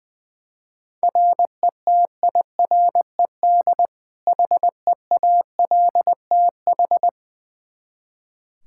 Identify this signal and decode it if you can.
Morse code, heard as a keyed tone, about 20 wpm, 700 Hz. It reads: RETIRED HEALTH
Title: Practice QSO - Exchange 3 - More Info - 20wpm